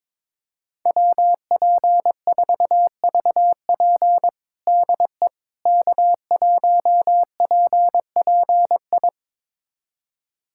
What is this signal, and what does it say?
Morse code, heard as a keyed tone, about 22 wpm, 700 Hz. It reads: WP4VP DE K1PPI
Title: USA Callsign DE Callsign - 5 Characters - 22wpm